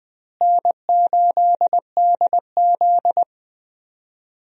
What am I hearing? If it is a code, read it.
Morse code, heard as a keyed tone, about 21 wpm, 700 Hz. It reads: N8DZ
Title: USA Call Signs - 4 Characters - Repeated 3x - 20wpm